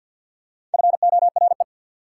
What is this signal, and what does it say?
Morse code, heard as a keyed tone, about 48 wpm, 690 Hz. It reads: FORE
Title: Prefixes and Suffixes 50wpm